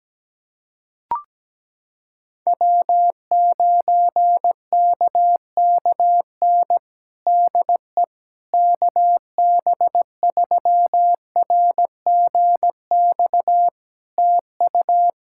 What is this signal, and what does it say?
Morse code, heard as a keyed tone, about 17 wpm, 700 Hz. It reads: W9KKN DE KB3RGX TU
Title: Practice QSO - Exchange 4 - Ending - 17wpm